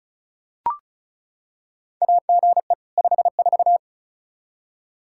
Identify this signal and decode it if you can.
Morse code, heard as a keyed tone, about 36 wpm, 700 Hz. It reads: AGE 54
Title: Practice QSO - Exchange 3 - More Info - 35wpm